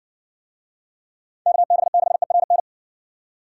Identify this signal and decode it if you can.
Morse code, heard as a keyed tone, about 60 wpm, 700 Hz. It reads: KB6ERN